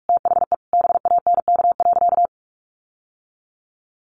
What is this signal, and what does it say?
Morse code, heard as a keyed tone, about 44 wpm, 700 Hz. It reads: THE BANK.
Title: Sentences from Top 1500 Words 45wpm